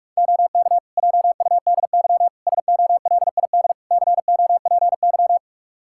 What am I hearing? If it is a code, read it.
Morse code, heard as a keyed tone, about 47 wpm, 700 Hz. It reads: OK JUDY SOLID COPY
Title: Practice QSO - Exchange 3 - More Info - 45wpm